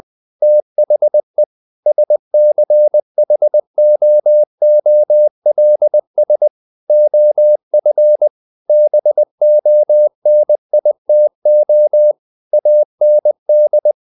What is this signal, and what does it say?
Morse code, heard as a keyed tone, about 20 wpm, 600 Hz. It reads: THE SCHOOLS OF BONITO AND
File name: the old man and the sea / the old man and the sea part 1 20wpm